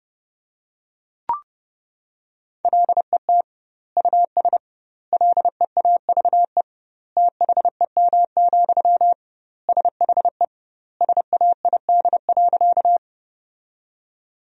Morse code, heard as a keyed tone, about 30 wpm, 700 Hz. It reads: LET US LEAVE THEM, SHE SAID.
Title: 2x Word-Spacing - Sentences from Top 400 Words - 30wpm